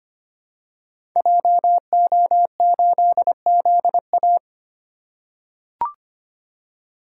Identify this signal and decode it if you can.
Morse code, heard as a keyed tone, about 25 wpm, 700 Hz. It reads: JO8ZA
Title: Code Groups - 5 Letter:Numbers - 25wpm